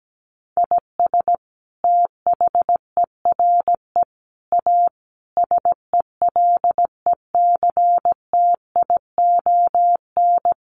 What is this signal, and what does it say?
Morse code, heard as a keyed tone, about 17 wpm, 700 Hz. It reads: IS THERE A SELECTION
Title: Sentences from Top 1400 Words 17wpm